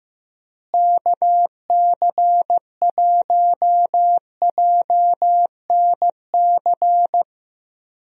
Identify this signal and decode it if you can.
Morse code, heard as a keyed tone, about 15 wpm, 700 Hz. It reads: KC1JNC
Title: USA Call Signs - 6 Characters - Repeated 2x - 15wpm